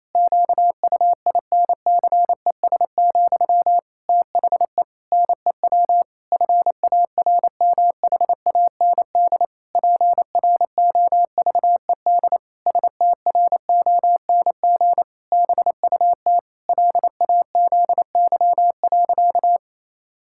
Morse code, heard as a keyed tone, about 28 wpm, 700 Hz. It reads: QUINCES, THE NEW FARMHAND PROVED STRONG BUT LAZY.
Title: Pangram Challenge v2 - 28wpm